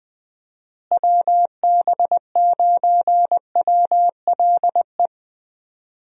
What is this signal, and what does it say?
Morse code, heard as a keyed tone, about 20 wpm, 700 Hz. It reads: WB9WLE
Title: USA Call Signs - 6 Characters - Repeated 3x - 20wpm